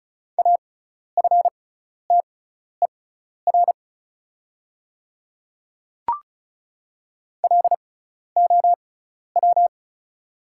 Morse code, heard as a keyed tone, about 34 wpm, spaced as for 12 wpm, 700 Hz. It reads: AFTER LOW
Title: 6x Character Spacing - Top 200 Words - 35wpm